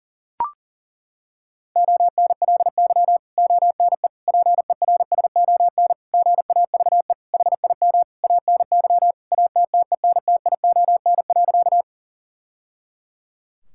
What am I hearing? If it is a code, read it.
Morse code, heard as a keyed tone, about 40 wpm, 700 Hz. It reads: ONLY ONE PERSON GAVE HIM ANY ATTENTION.
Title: Sentences from Top 1000 Words - Review Part 6 - 40wpm